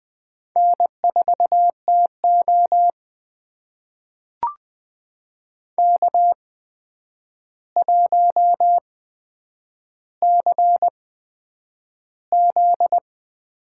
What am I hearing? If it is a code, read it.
Morse code, heard as a keyed tone, about 20 wpm, spaced as for 9 wpm, 700 Hz. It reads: N4TO K1CZ
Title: USA Call Signs - 4 Characters - 8x Character Spacing - 20wpm